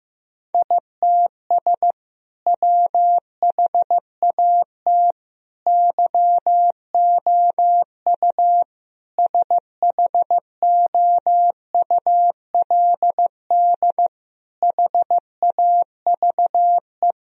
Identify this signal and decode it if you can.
Morse code, heard as a keyed tone, about 15 wpm, 700 Hz. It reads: ITS WHAT YOU SHOULD HAVE
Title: Sentences from Top 1000 Words - Review Part 3 - 15wpm